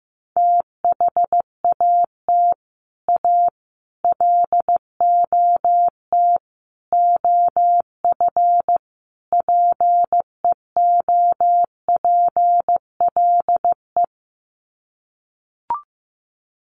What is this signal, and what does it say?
Morse code, heard as a keyed tone, about 15 wpm, 700 Hz. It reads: THAT A LOT OF PEOPLE
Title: Sets of 5 Words 15wpm